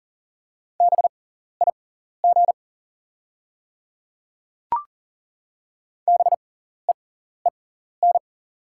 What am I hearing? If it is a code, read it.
Morse code, heard as a keyed tone, about 40 wpm, spaced as for 14 wpm, 700 Hz. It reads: BIG BEEN
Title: 6x Character Spacing - Top 200 Words - 40wpm